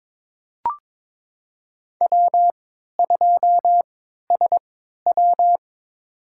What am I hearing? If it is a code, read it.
Morse code, heard as a keyed tone, about 22 wpm, spaced as for 12 wpm, 700 Hz. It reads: W2SW
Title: USA Call Signs - 4 Characters - 3x Character Spacing - 22wpm